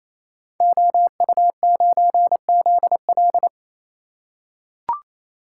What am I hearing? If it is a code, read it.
Morse code, heard as a keyed tone, about 28 wpm, 700 Hz. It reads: OU9ZL
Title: Code Groups - 5 Letter:Numbers - 28wpm